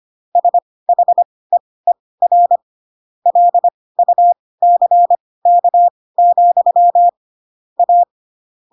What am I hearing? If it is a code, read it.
Morse code, heard as a keyed tone, about 25 wpm, spaced as for 18 wpm, 700 Hz. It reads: SHEER LUCK, A
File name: Pinocchio ch3 12 25 ik1hge